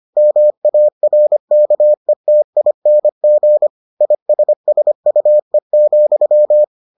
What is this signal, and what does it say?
Morse code, heard as a keyed tone, about 25 wpm, 600 Hz. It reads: MARKETING ISSUE,